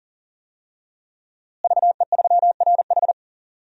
Morse code, heard as a keyed tone, about 40 wpm, 700 Hz. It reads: VE3RH
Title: DXCC-100 Call Signs - 5 Characters - Repeated 2x - 40wpm